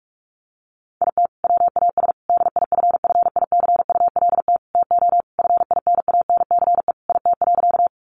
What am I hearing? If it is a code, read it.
Morse code, heard as a keyed tone, about 44 wpm, 700 Hz. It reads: IT WAS DIFFICULT TO FINANCE IT.
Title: Sentences from Top 1500 Words 45wpm